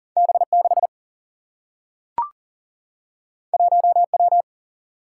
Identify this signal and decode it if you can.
Morse code, heard as a keyed tone, about 40 wpm, 700 Hz. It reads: B 6 1 W
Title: Single-Letter Single-Number 40wpm